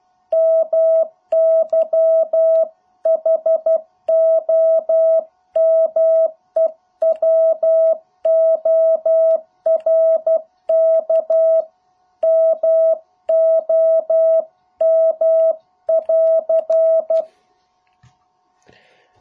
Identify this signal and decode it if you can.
Morse code, heard as a keyed tone, about 12 wpm, 650 Hz. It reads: MY HOMEWORK MOM <AR>